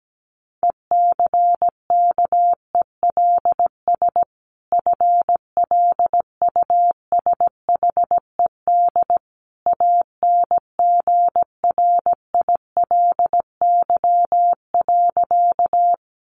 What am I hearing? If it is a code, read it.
Morse code, heard as a keyed tone, about 17 wpm, 700 Hz. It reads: ECKELS FLUSHED ANGRILY.
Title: A Sound Of Thunder 17wpm